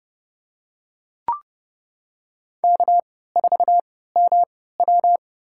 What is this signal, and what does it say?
Morse code, heard as a keyed tone, about 30 wpm, spaced as for 17 wpm, 700 Hz. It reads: K4MW
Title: USA Call Signs - 4 Characters - 3x Character Spacing - 30wpm